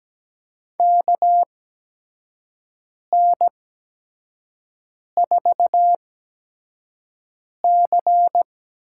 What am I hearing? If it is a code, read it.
Morse code, heard as a keyed tone, about 17 wpm, spaced as for 5 wpm, 700 Hz. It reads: KN4C
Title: USA Call Signs - 8x Character Spacing - 17wpm